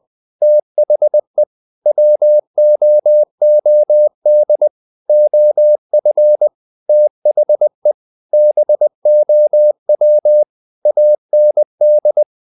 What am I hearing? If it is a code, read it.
Morse code, heard as a keyed tone, about 20 wpm, 600 Hz. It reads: THE WOOD OF THE BOW AND
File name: the old man and the sea / the old man and the sea part 1 20wpm